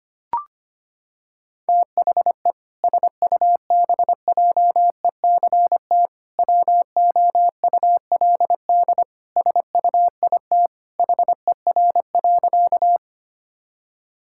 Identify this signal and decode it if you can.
Morse code, heard as a keyed tone, about 25 wpm, 700 Hz. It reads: THE SUBJECT WOULD SUIT HER.
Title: Sentences from Top 1600 Words 25wpm